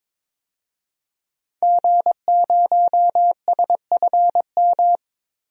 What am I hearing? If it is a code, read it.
Morse code, heard as a keyed tone, about 22 wpm, 700 Hz. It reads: G0SFM
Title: DXCC-100 Call Signs - 5 Characters - Repeated 2x - 22wpm